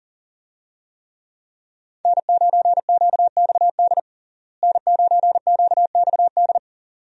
Morse code, heard as a keyed tone, about 40 wpm, 700 Hz. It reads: N9QXD N9QXD
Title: USA Call Signs - 5 Characters - Repeated 2x - 40wpm